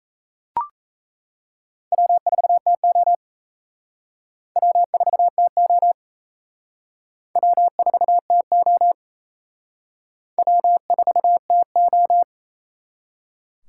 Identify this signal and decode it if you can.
Morse code, heard as a keyed tone, about 35 wpm, 700 Hz. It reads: W4TO W4TO W4TO W4TO
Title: Speed Racing - USA 1x2 Call Signs 28wpm